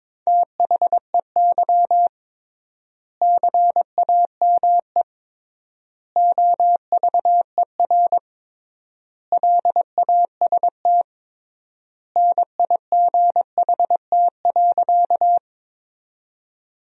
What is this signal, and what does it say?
Morse code, heard as a keyed tone, about 22 wpm, 700 Hz. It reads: THEY CAME OVER LAST NIGHT.
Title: 3x Word-Spacing - Sentences from Top 300 Words - 22wpm